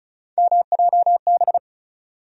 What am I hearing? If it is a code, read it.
Morse code, heard as a keyed tone, about 34 wpm, 700 Hz. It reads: MJB
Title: USA Call Sign Postfixes - 3 Letters 35wpm